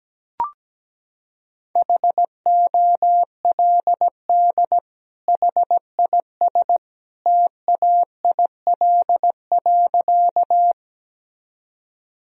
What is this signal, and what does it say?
Morse code, heard as a keyed tone, about 17 wpm, 700 Hz. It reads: HOLD HIS TAIL.
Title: Sentences from Top 1000 Words - Review Part 3 - 17wpm